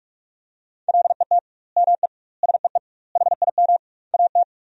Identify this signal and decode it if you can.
Morse code, heard as a keyed tone, about 42 wpm, 700 Hz. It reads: LET ME SEE HIM AT
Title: 2x Word-Spacing - Sentences from Top 300 Words - 45wpm